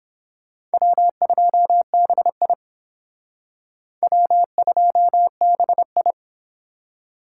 Morse code, heard as a keyed tone, about 29 wpm, 700 Hz. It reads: W2BI W2BI